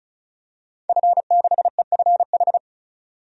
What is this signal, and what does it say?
Morse code, heard as a keyed tone, about 34 wpm, 700 Hz. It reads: F6EFH
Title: DXCC-100 Call Signs - 5 Characters - 35wpm